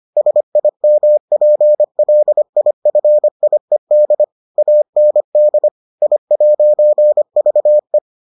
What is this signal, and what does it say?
Morse code, heard as a keyed tone, about 25 wpm, 600 Hz. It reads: SIMPLIFIED AND I'VE